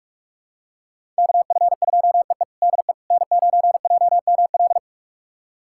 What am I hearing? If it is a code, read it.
Morse code, heard as a keyed tone, about 47 wpm, 700 Hz. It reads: KF2EE DE N9JML
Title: USA Callsign DE Callsign - 5 Characters - 45wpm